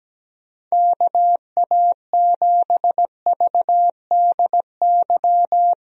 Morse code, heard as a keyed tone, about 17 wpm, 700 Hz. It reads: KA7VDY
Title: USA Call Signs - 6 Characters - Repeated 2x - 17wpm